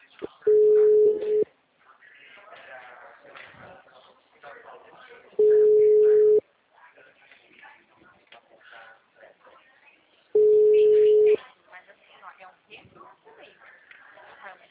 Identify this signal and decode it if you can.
Ringback tone (Europe, 425 Hz), with speech.
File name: out- -2017- - - .39286